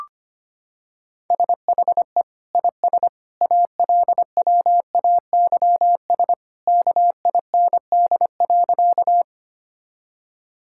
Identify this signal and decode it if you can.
Morse code, heard as a keyed tone, about 25 wpm, 700 Hz. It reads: SHE IS ALWAYS KIND.
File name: practice / 25wpm / Sentences with 2-Word Phrases - 25wpm